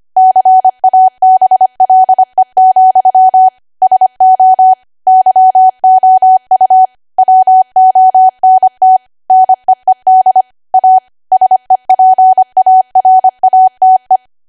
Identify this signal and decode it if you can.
Morse code, heard as a keyed tone, about 25 wpm, 750 Hz. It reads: CABLE, SO YOU WONT NEED A SEPARATE
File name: cwa / W1AW-Level3 / 25WPM